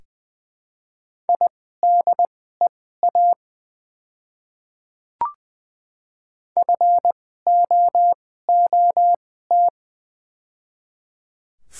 Morse code, heard as a keyed tone, about 20 wpm, spaced as for 14 wpm, 700 Hz. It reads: IDEA FOOT
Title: 2x Character Spacing - Top 500 Words - 20wpm